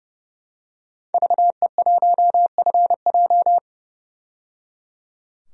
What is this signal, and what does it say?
Morse code, heard as a keyed tone, about 29 wpm, 700 Hz. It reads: VE1FJ
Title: DXCC-100 Call Signs - 5 Characters - 30wpm